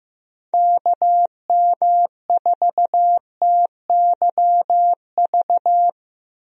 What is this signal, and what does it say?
Morse code, heard as a keyed tone, about 15 wpm, 700 Hz. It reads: KM4TYV
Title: USA Call Signs - 6 Characters - Repeated 3x - 15wpm